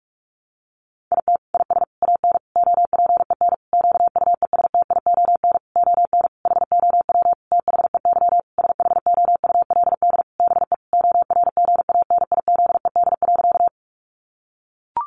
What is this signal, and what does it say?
Morse code, heard as a keyed tone, about 44 wpm, 700 Hz. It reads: IT IS AN OPEN QUESTION ON HOW THEY SHOULD BE ORGANIZED.
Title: Sentences from Top 1300 Words 45wpm